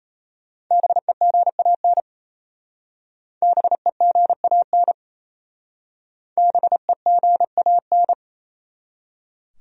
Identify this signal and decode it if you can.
Morse code, heard as a keyed tone, about 37 wpm, 700 Hz. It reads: BEGAN BEGAN BEGAN